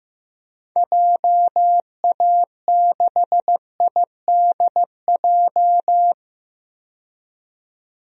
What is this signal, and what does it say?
Morse code, heard as a keyed tone, about 15 wpm, 700 Hz. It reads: JA6IDJ
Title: DXCC-100 Call Signs - 6 Characters - 15wpm